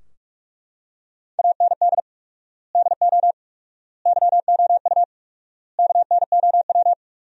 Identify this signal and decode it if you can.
Morse code, heard as a keyed tone, about 43 wpm, 700 Hz. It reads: AND DO YOU KNOW
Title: 4x Word-Spacing - Sentences from Top 200 Words - 45wpm